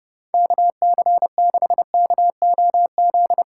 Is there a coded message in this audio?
Yes: Morse code from 0.3 to 3.4 s, heard as a keyed tone, about 38 wpm, 700 Hz. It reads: KC6KOZ